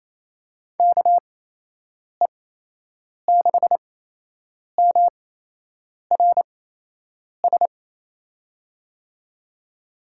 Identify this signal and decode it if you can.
Morse code, heard as a keyed tone, about 28 wpm, spaced as for 8 wpm, 700 Hz. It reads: KE6MRS
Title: USA Call Signs - 6 Characters - 8x Character Spacing - 28wpm